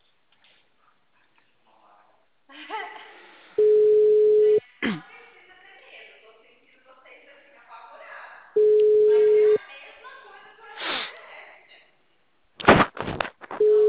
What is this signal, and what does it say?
Ringback tone (Europe, 425 Hz), with speech.